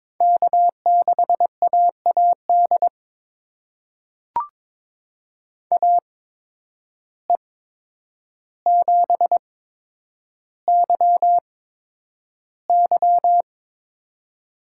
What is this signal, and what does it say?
Morse code, heard as a keyed tone, about 22 wpm, spaced as for 9 wpm, 700 Hz. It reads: K6AAD AE7YY
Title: USA Call Signs - 5 Characters - 8x Character Spacing - 22wpm